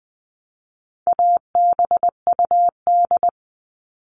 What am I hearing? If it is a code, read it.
Morse code, heard as a keyed tone, about 20 wpm, 700 Hz. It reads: ABUD
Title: All Cut Numbers - 4-Digit 20wpm